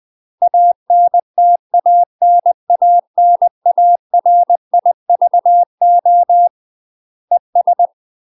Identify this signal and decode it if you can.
Morse code, heard as a keyed tone, about 20 wpm, 700 Hz. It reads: ANTANANARIVO ES
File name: UK MASTER QSOs TRAINING / 20WPM / UK MASTER QSOs 0025